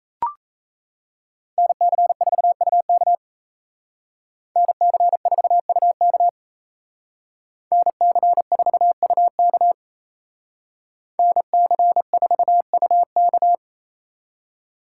Morse code, heard as a keyed tone, about 36 wpm, 700 Hz. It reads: NC4UK NC4UK NC4UK NC4UK